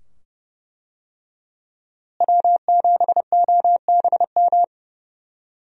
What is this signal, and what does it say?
Morse code, heard as a keyed tone, about 32 wpm, 700 Hz. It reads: W7OBM
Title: USA Call Signs - 5 Characters - Repeated 2x - 30wpm